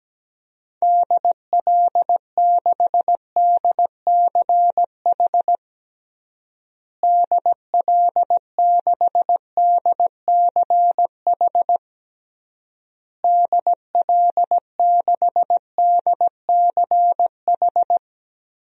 Morse code, heard as a keyed tone, about 17 wpm, 700 Hz. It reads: DL6DCH DL6DCH DL6DCH
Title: DXCC-100 Call Signs - 6 Characters - Repeated 3x - 17wpm